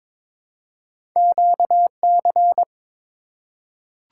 Morse code, heard as a keyed tone, about 22 wpm, 700 Hz. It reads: QC
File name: practice / 22wpm / ARRL Sections v2 - 22wpm